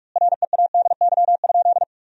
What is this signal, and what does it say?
Morse code, heard as a keyed tone, about 41 wpm, 700 Hz. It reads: READY?
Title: 3x Word-Spacing - Sentences from Top 400 Words - 45wpm